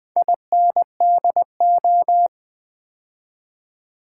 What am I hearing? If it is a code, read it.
Morse code, heard as a keyed tone, about 20 wpm, 700 Hz. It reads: INDO
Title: Build A Word - 2 Letters At A Time - 6-Letter Words - 20wpm